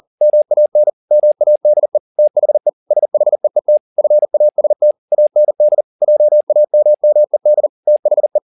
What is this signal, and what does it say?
Morse code, heard as a keyed tone, about 40 wpm, 600 Hz. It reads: MAN MADE THE SHEET FAST AND JAMMED THE
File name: the old man and the sea 2 40wpm